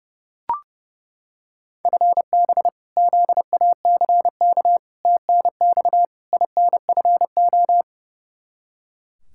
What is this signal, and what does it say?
Morse code, heard as a keyed tone, about 30 wpm, 700 Hz. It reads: FB ZACK TNX INFO
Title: Practice QSO - Exchange 3 - More Info - 30wpm